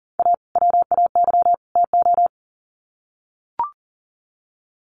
Morse code, heard as a keyed tone, about 40 wpm, 700 Hz. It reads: A WAY TO
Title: Sets of 3 Words 40wpm